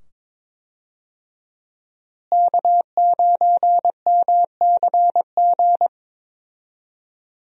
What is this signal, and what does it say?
Morse code, heard as a keyed tone, about 22 wpm, 700 Hz. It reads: K9MCG
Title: USA Call Signs - 5 Characters - Repeated 2x - 22wpm